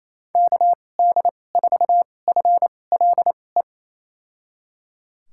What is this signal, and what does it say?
Morse code, heard as a keyed tone, about 28 wpm, spaced as for 20 wpm, 700 Hz. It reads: KD4FLE
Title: USA Call Signs - 6 Characters - 2x Character Spacing - 28wpm